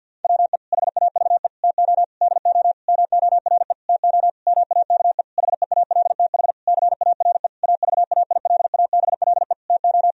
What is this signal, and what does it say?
Morse code, heard as a keyed tone, about 50 wpm, 700 Hz. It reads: WE HAVE TO DO MORE TO MAKE HEALTH CARE AVAILABLE TO
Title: Sentences from Top 1000 Words - Review Part 3 - 50wpm